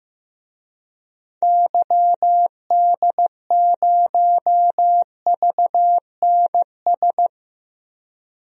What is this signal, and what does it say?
Morse code, heard as a keyed tone, about 15 wpm, 690 Hz. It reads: YD0VNS
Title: DXCC-100 Call Signs - 6 Characters - 15wpm